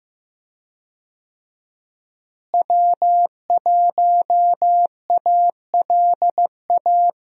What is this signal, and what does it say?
Morse code, heard as a keyed tone, about 15 wpm, 700 Hz. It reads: W1ALA
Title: USA Call Signs - 5 Characters - Repeated 3x - 15wpm